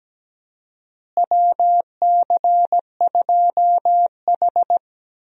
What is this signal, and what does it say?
Morse code, heard as a keyed tone, about 17 wpm, 700 Hz. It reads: WC2H
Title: USA Call Signs - 4 Characters - Repeated 2x - 17wpm